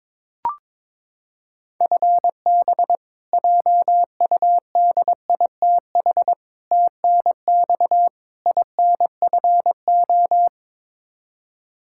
Morse code, heard as a keyed tone, about 22 wpm, 700 Hz. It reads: FB JUDITH TNX INFO
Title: K1USN SST QSO Practice 22wpm